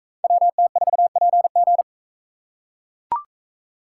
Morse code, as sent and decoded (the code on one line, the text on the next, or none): .-- - ....- .--. --.
WT4PG